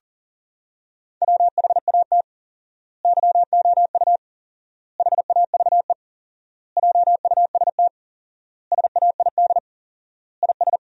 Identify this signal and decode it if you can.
Morse code, heard as a keyed tone, about 40 wpm, 700 Hz. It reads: WHAT YOU HAVE JUST SAID IS
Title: 4x Word-Spacing - Sentences from Top 200 Words - 40wpm